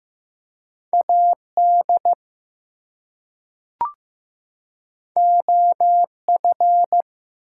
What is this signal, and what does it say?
Morse code, heard as a keyed tone, about 15 wpm, 700 Hz. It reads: AD OF
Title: Two Letter Words - 15wpm